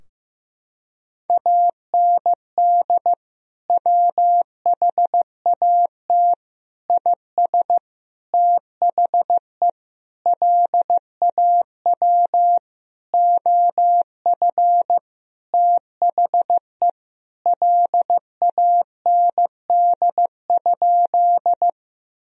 Morse code, heard as a keyed tone, about 16 wpm, 700 Hz. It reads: AND WHAT IS THE LAW OF THE LAND?
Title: Sentences from Top 1000 Words - Review Part 3 - 15wpm